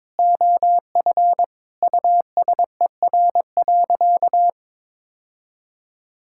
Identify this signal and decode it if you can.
Morse code, heard as a keyed tone, about 22 wpm, 700 Hz. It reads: OF USER.
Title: Sentences from Top 1000 Words - Review Part 5 - 22wpm